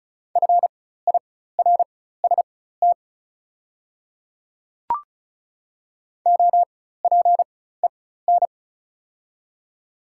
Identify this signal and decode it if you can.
Morse code, heard as a keyed tone, about 35 wpm, spaced as for 16 wpm, 700 Hz. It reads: FIRST OPEN
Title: 4x Character Spacing - Top 300 Words - 35wpm